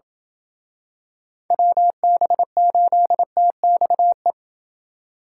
Morse code, heard as a keyed tone, about 28 wpm, 700 Hz. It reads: WB8TXE